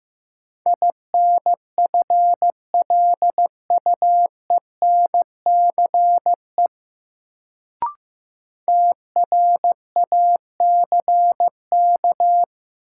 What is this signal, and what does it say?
Morse code, heard as a keyed tone, about 15 wpm, 700 Hz. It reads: INFLUENCE TRACK
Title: Sentences from Top 1100 Words - New Word Drill - 15wpm